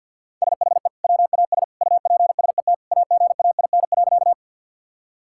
Morse code, heard as a keyed tone, about 52 wpm, 700 Hz. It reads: SHE WAS UPSET AGAIN.
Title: Sentences from Top 1000 Words - Review Part 5 - 50wpm